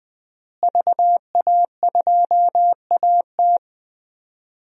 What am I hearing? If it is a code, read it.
Morse code, heard as a keyed tone, about 20 wpm, 700 Hz. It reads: VA2AT
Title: DXCC-100 Call Signs - 5 Characters - Repeated 2x - 20wpm